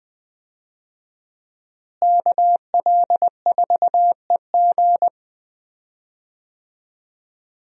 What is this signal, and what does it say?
Morse code, heard as a keyed tone, about 20 wpm, 700 Hz. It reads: KL4EG
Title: USA Call Signs - 5 Characters - Repeated 2x - 20wpm